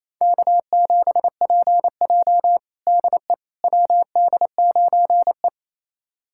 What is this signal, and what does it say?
Morse code, heard as a keyed tone, about 28 wpm, 700 Hz. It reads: K7PJ DE WD9E
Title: USA Callsign DE Callsign - 4 Characters - 28wpm